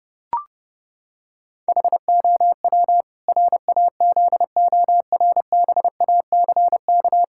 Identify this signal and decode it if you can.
Morse code, heard as a keyed tone, about 30 wpm, 700 Hz. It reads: HOW RAZORBACK
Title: Pangram Challenge v2 - 30wpm